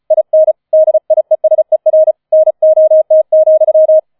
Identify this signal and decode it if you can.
Morse code, heard as a keyed tone, about 35 wpm, 630 Hz. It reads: IN DIESER NOT,
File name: 005 Fünfter Streich 35wpm